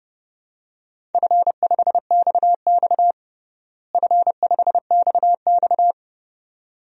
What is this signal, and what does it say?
Morse code, heard as a keyed tone, about 30 wpm, 700 Hz. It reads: F5XX F5XX
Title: DXCC-100 Call Signs - 4 Characters - Repeated 2x - 30wpm